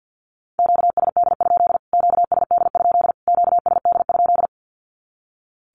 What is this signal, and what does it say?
Morse code, heard as a keyed tone, about 50 wpm, 700 Hz. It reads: QSD? QSD? QSD?
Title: Q-Codes Repeated 3x 50wpm